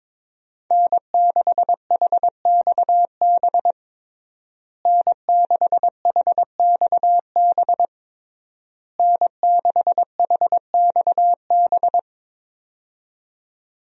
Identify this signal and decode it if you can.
Morse code, heard as a keyed tone, about 22 wpm, 700 Hz. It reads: N6HXB N6HXB N6HXB
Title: USA Call Signs - 5 Characters - Repeated 3x - 22wpm